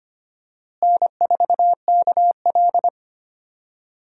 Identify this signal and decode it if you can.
Morse code, heard as a keyed tone, about 25 wpm, 700 Hz. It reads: N4KL
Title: USA Call Signs - 4 Characters - Repeated 2x - 25wpm